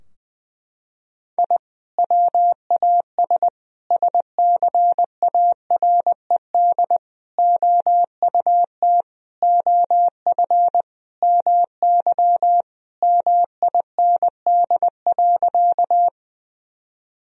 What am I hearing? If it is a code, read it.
Morse code, heard as a keyed tone, about 21 wpm, 700 Hz. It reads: I WAS SCARED OUT OF MY MIND.
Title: Sentences from Top 1000 Words - Review Part 4 - 20wpm